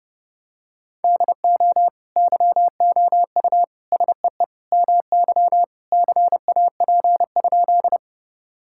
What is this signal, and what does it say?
Morse code, heard as a keyed tone, about 32 wpm, 700 Hz. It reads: DO YOU SEE MY CAP?
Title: Sentences from Top 1600 Words 30wpm